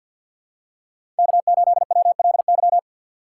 Morse code, heard as a keyed tone, about 50 wpm, 700 Hz. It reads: K8WLY
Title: USA Call Signs - 5 Characters - 6x Character Spacing - 50wpm